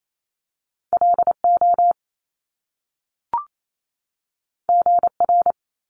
Morse code, heard as a keyed tone, about 28 wpm, 700 Hz. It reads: LO GR
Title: Top Two Letter Combinations in Words 28wpm